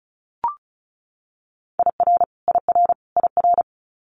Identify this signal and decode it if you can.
Morse code, heard as a keyed tone, about 36 wpm, 700 Hz. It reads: IR IR IR
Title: Multi-letter Phonemes Repeated 3x 35wpm